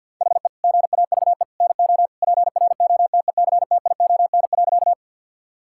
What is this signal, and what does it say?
Morse code, heard as a keyed tone, about 50 wpm, 700 Hz. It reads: HE GAVE NO PROTECTION.
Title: Sentences from Top 1600 Words 50wpm